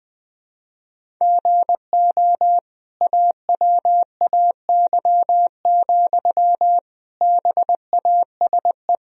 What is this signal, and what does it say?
Morse code, heard as a keyed tone, about 20 wpm, 700 Hz. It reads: GO AWAY, BASE
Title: Sentences from Top 1000 Words - Review Part 1 - 20wpm